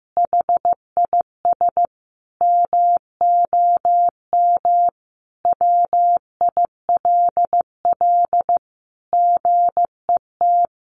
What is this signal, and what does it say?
Morse code, heard as a keyed tone, about 15 wpm, 700 Hz. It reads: HIS MOM WILL GET
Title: Additional Easy Phrases 15wpm